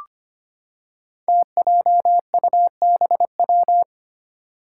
Morse code, heard as a keyed tone, about 25 wpm, 700 Hz. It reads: TJUBW
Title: Five Challenging Letter-Number Combinations 25wpm